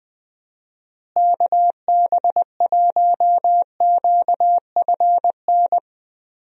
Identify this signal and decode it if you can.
Morse code, heard as a keyed tone, about 21 wpm, 700 Hz. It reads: KB1QFN
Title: USA Call Signs - 6 Characters - Repeated 2x - 20wpm